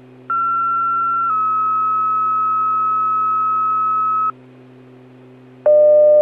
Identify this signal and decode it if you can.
Two-tone page, heard as a one-second tone, then a three-second tone, about 1360 then 1290 Hz. It is heard from 0.3 to 4.3 s.